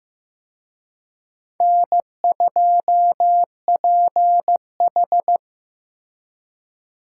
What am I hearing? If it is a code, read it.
Morse code, heard as a keyed tone, about 18 wpm, 700 Hz. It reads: N2PH
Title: Speed Racing - USA 1x2 Call Signs 15wpm